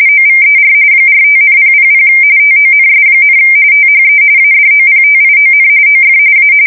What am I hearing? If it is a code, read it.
RTTY, about 45.45 baud. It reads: OVER 45,000 TONS, THE ADOPTION OF A
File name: fsk170 / USS-New-Jersey